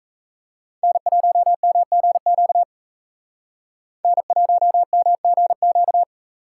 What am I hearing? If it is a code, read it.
Morse code, heard as a keyed tone, about 34 wpm, 700 Hz. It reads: N1MGQ N1MGQ
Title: Speed Racing - USA 1x3 Call Signs 28wpm